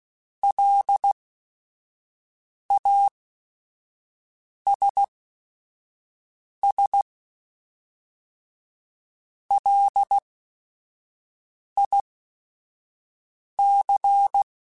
Morse code, heard as a keyed tone, about 16 wpm, spaced as for 5 wpm, 800 Hz. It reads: LASS LIC